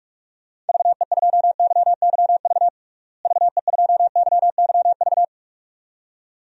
Morse code, heard as a keyed tone, about 42 wpm, 700 Hz. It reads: VE2YYV VE2YYV
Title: DXCC-100 Call Signs - 6 Characters - Repeated 2x - 45wpm